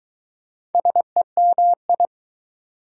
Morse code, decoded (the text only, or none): SEMI